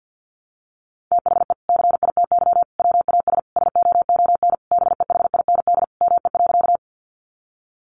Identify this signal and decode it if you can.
Morse code, heard as a keyed tone, about 50 wpm, 700 Hz. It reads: THE CITY WAS SOON BEHIND ME.
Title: Sentences from Top 500 Words 50wpm